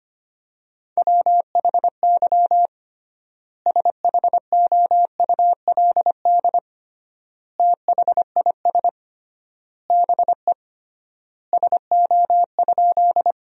Morse code, heard as a keyed tone, about 25 wpm, 700 Hz. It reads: WHY SHOULD THIS BE SO?
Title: 3x Word-Spacing - Sentences from Top 300 Words - 25wpm